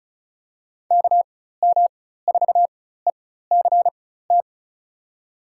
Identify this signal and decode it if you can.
Morse code, heard as a keyed tone, about 34 wpm, spaced as for 16 wpm, 700 Hz. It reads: KM4ECT
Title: USA Call Signs - 6 Characters - 4x Character Spacing - 35wpm